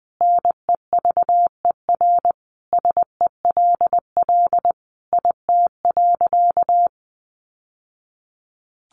Morse code, heard as a keyed tone, about 20 wpm, 700 Hz. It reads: NEVER SELL IT.
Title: Sentences from Top 1100 Words 20wpm